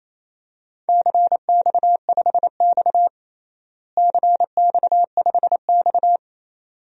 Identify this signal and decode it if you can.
Morse code, heard as a keyed tone, about 27 wpm, 700 Hz. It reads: CX5X CX5X
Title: DXCC-100 Call Signs - 4 Characters - Repeated 2x - 28wpm